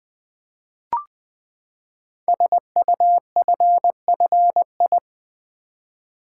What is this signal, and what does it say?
Morse code, heard as a keyed tone, about 20 wpm, 700 Hz. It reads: SUFFI
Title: Build A Word - 5 Letters At A Time - 12-Letter Words - 20wpm